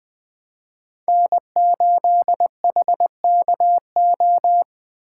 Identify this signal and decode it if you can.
Morse code, heard as a keyed tone, about 20 wpm, 700 Hz. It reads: N8HKO
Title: Speed Racing - USA 1x3 Call Signs 17wpm